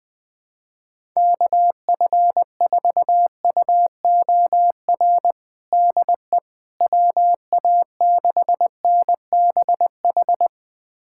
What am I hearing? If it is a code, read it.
Morse code, heard as a keyed tone, about 20 wpm, 700 Hz. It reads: KF4UOR DE WA6NBH
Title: USA Callsign DE Callsign - 6 Characters - 20wpm